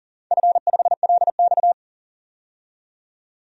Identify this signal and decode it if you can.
Morse code, heard as a keyed tone, about 40 wpm, 700 Hz. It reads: F5LX
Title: DXCC-100 Call Signs - 4 Characters - 40wpm